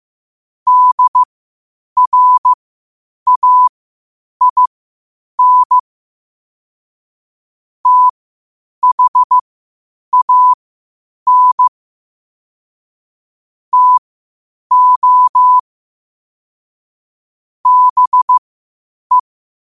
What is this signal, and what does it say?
Morse code, heard as a keyed tone, about 15 wpm, spaced as for 8 wpm, 1000 Hz. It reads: DRAIN THAN TO BE
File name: 8wpm